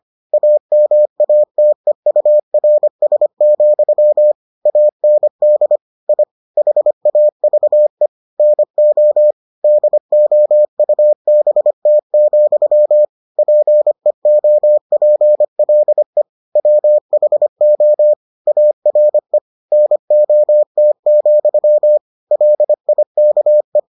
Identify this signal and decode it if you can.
Morse code, heard as a keyed tone, about 25 wpm, 600 Hz. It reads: AMATEURS, AND I HAVE NO DOUBT, PEOPLE WHO ARE NOT, LIKE